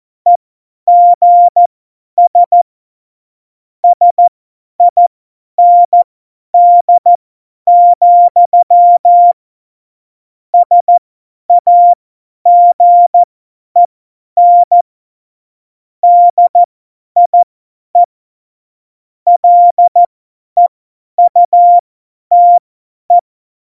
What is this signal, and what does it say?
Morse code, heard as a keyed tone, about 14 wpm, spaced as for 10 wpm, 700 Hz. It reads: EGS SIND, SAGEN DIE LEUTE
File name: Morsekurs / Witze / Witz 27. Oktober 2020, 50 F70